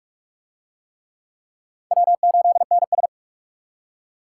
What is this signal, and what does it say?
Morse code, heard as a keyed tone, about 39 wpm, 700 Hz. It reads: W8NS